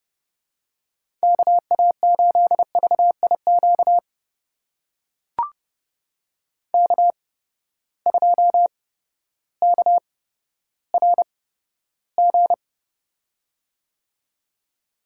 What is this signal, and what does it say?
Morse code, heard as a keyed tone, about 30 wpm, spaced as for 14 wpm, 700 Hz. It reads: KA8VIQ K2KRG
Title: USA Call Signs - 8x Character Spacing - 30wpm